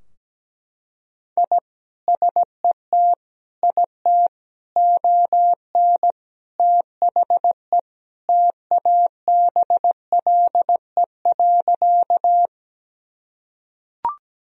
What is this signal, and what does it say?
Morse code, heard as a keyed tone, about 17 wpm, 700 Hz. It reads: I SET IT ON THE TABLE.
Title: Sentences from Top 1000 Words - Review Part 2 - 17wpm